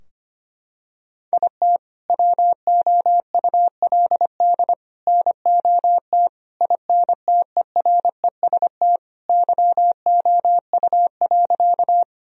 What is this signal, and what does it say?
Morse code, heard as a keyed tone, about 25 wpm, 700 Hz. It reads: IT WOULD NOT INTEREST YOU.
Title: Sentences from Top 1000 Words - Review Part 2 - 25wpm